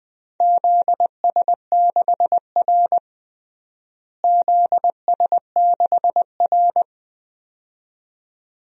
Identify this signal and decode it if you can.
Morse code, heard as a keyed tone, about 20 wpm, 700 Hz. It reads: ZS6R ZS6R
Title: DXCC-100 Call Signs - 4 Characters - Repeated 2x - 20wpm